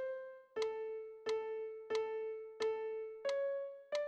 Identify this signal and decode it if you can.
There is no code.